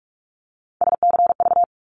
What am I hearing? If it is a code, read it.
Morse code, heard as a keyed tone, about 44 wpm, 700 Hz. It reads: SCV
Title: ARRL Sections 45wpm